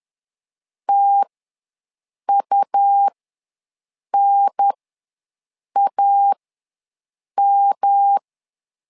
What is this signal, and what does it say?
Morse code, heard as a keyed tone, about 10 wpm, spaced as for 6 wpm, 800 Hz. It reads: TUNAM
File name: LEZIONI / 3 MWRO / 3 M ws 10wpm 2121